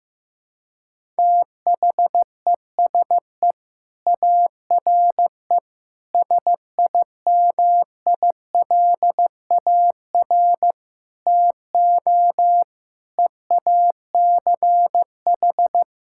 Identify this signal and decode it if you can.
Morse code, heard as a keyed tone, about 15 wpm, 700 Hz. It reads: THESE ARE SIMILAR TO EACH
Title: Sentences from Top 1000 Words - Review Part 5 - 15wpm